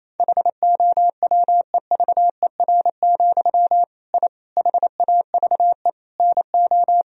Morse code, heard as a keyed tone, about 28 wpm, 700 Hz. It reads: HOWEVER, I HAVE NO
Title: Sentences from Top 1000 Words - Review Part 3 - 28wpm